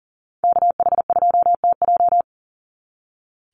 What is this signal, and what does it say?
Morse code, heard as a keyed tone, about 40 wpm, 700 Hz. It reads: KH2TJ
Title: NA SOTA Activator Callsigns 40wpm